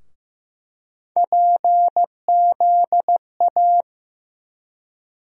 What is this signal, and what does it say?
Morse code, heard as a keyed tone, about 15 wpm, 700 Hz. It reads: PZA
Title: USA Call Sign Postfixes - 3 Letters 15wpm